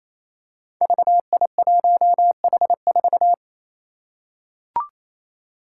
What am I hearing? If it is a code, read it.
Morse code, heard as a keyed tone, about 28 wpm, 700 Hz. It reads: VI1H4
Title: Five Challenging Letter-Number Combinations 28wpm